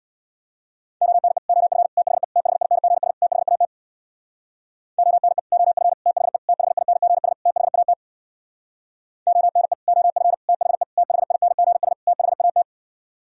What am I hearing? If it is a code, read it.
Morse code, heard as a keyed tone, about 64 wpm, 690 Hz. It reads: ONE OF THE THINGS THAT ONE OF THE THINGS THAT ONE OF THE THINGS THAT